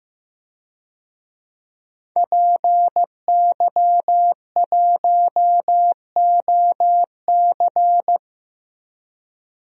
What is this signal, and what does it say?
Morse code, heard as a keyed tone, about 15 wpm, 700 Hz. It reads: PY1OC
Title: DXCC-100 Call Signs - 5 Characters - 15wpm